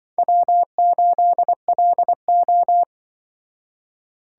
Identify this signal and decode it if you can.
Morse code, heard as a keyed tone, about 24 wpm, 700 Hz. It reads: W8LO
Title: Speed Racing - USA 1x2 Call Signs 20wpm